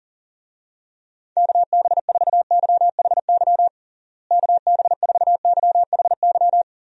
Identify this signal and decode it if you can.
Morse code, heard as a keyed tone, about 40 wpm, 700 Hz. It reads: KB4YHY KB4YHY
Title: USA Call Signs - 6 Characters - Repeated 3x - 40wpm